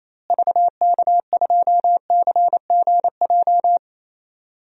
Morse code, heard as a keyed tone, about 28 wpm, 700 Hz. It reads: VK2CGJ
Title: DXCC-100 Call Signs - 6 Characters - Repeated 3x - 28wpm